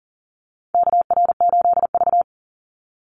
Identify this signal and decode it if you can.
Morse code, heard as a keyed tone, about 38 wpm, 700 Hz. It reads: KR8V
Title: Call Signs - 40wpm